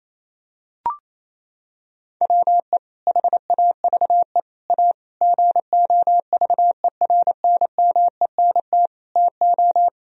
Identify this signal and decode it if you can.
Morse code, heard as a keyed tone, about 28 wpm, 700 Hz. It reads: WE HAVE A GOVERNMENT TO
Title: Sentences from Top 1000 Words - Review Part 1 - 28wpm